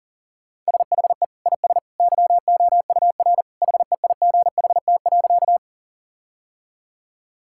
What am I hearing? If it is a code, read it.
Morse code, heard as a keyed tone, about 40 wpm, 700 Hz. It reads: SHE IS YOUR HEIGHT.
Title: Sentences from Top 1600 Words 40wpm